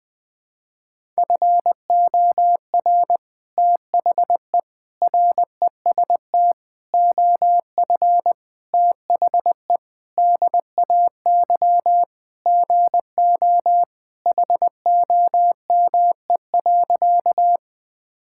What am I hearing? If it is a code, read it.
Morse code, heard as a keyed tone, about 20 wpm, 700 Hz. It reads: FOR THE REST OF THE DAY GO HOME.
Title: Sentences with 3-Word Phrases - 20wpm